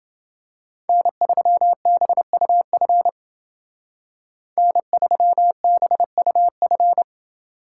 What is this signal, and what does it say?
Morse code, heard as a keyed tone, about 29 wpm, 700 Hz. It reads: N3BUF N3BUF